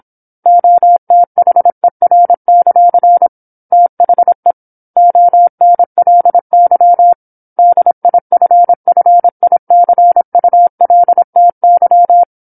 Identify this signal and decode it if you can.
Morse code, heard as a keyed tone, about 27 wpm, 700 Hz. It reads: OTHER; THE ONLY DIFFICULTY
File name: 08-Chapter-08 26 WPM